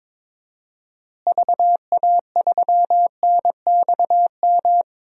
Morse code, heard as a keyed tone, about 22 wpm, 700 Hz. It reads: VA3NXM
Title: DXCC-100 Call Signs - 6 Characters - Repeated 2x - 22wpm